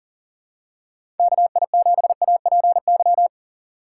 Morse code, heard as a keyed tone, about 38 wpm, 690 Hz. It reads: KI7APY
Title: USA Call Signs - 4x Character Spacing - 40wpm